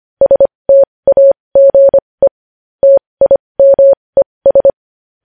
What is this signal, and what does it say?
Morse code, heard as a keyed tone, about 25 wpm, spaced as for 20 wpm, 550 Hz. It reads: STAGE TIMES
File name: podcast / News headlines-1x-25x20